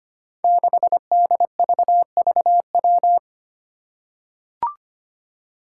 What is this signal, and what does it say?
Morse code, heard as a keyed tone, about 25 wpm, 700 Hz. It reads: 6DVVW
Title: Five Challenging Letter-Number Combinations 25wpm